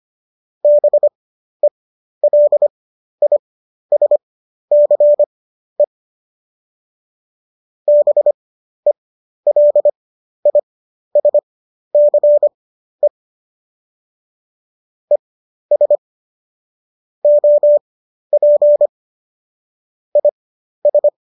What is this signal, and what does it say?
Morse code, heard as a keyed tone, about 25 wpm, spaced as for 12 wpm, 590 Hz. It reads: BELISCE BELISCE ES OP IS
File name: qso012 12